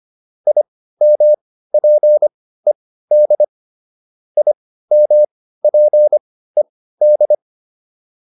Morse code, heard as a keyed tone, about 25 wpm, spaced as for 15 wpm, 600 Hz. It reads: IMPED IMPED